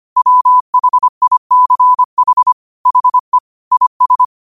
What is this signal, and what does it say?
Morse code, heard as a keyed tone, about 25 wpm, 1000 Hz. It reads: WHICH HE IS
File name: qotdfiles / 25wpm